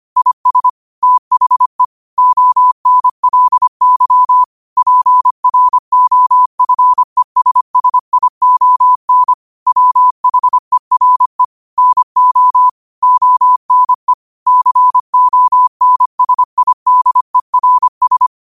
Morse code, heard as a keyed tone, about 25 wpm, 1000 Hz. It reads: IS THE ONLY PROFESSION WHERE NO ONE CONSIDERS